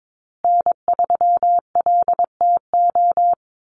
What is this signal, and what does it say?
Morse code, heard as a keyed tone, about 22 wpm, 700 Hz. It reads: N3LTO
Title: USA Call Signs Repeated 3x 22wpm